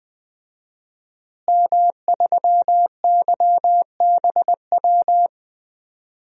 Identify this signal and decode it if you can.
Morse code, heard as a keyed tone, about 20 wpm, 700 Hz. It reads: M3YBW
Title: DXCC-100 Call Signs - 5 Characters - Repeated 3x - 20wpm